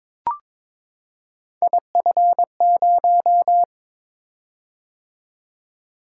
Morse code, heard as a keyed tone, about 21 wpm, 700 Hz. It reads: IF0
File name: practice / 22wpm / Three Letter-Number Combinations - 22wpm